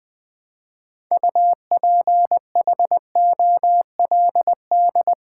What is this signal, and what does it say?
Morse code, heard as a keyed tone, about 20 wpm, 700 Hz. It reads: UPHOLD
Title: Build A Word - 2 Letters At A Time - 6-Letter Words - 20wpm